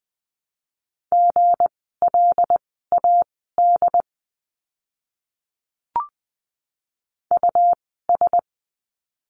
Morse code, heard as a keyed tone, about 20 wpm, spaced as for 15 wpm, 700 Hz. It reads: GLAD US
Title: 160 Lesson 26f - Letter g - Words 20wpm